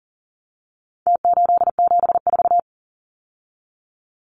Morse code, heard as a keyed tone, about 40 wpm, 700 Hz. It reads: T874
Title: Numbers - Contest Style - 4-Digit 40wpm